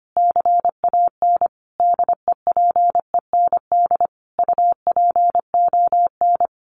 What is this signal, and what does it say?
Morse code, heard as a keyed tone, about 25 wpm, 700 Hz. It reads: CAN DEPEND UPON
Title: Sentences from Top 1300 Words 25wpm